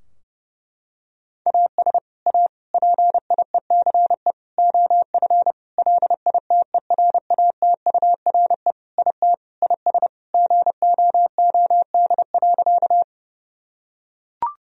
Morse code, heard as a keyed tone, about 32 wpm, 700 Hz. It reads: AS A PIECE OF LITERATURE IT IS GOOD.
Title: Sentences from Top 1000 Words - Review Part 3 - 30wpm